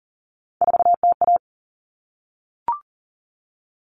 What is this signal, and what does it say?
Morse code, heard as a keyed tone, about 40 wpm, 700 Hz. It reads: 4TA
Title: All Cut Numbers - 3-Digit 40wpm